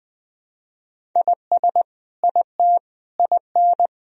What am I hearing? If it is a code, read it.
Morse code, heard as a keyed tone, about 20 wpm, 700 Hz. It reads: IS IT IN
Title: Sentences from Top 1000 Words - Review Part 1 - 20wpm